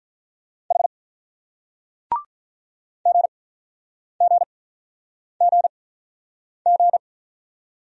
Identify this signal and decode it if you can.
Morse code, heard as a keyed tone, about 43 wpm, spaced as for 9 wpm, 700 Hz. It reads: H GGGG
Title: Speed-Racing Single Letter-Number 35wpm